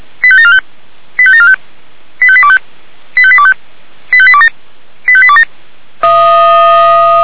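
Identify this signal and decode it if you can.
Five-tone selective call (ZVEI), heard as quick runs of five tones, tones of about 70 ms. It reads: 85745 85745 85825 85825 85827 85827
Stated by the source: Five-tone ZVEI selective call received by SDR, archived by the fire brigade Feuerwehr Orlamünde.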